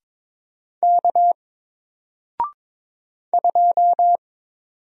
Morse code, heard as a keyed tone, about 22 wpm, 700 Hz. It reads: K 2
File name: Single-Letter-Number-22wpm